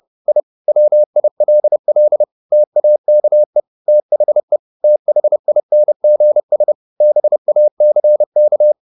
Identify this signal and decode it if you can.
Morse code, heard as a keyed tone, about 30 wpm, 600 Hz. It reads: I WILL TAKE THE THINGS BACK
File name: the old man and the sea / the old man and the sea 1 30wpm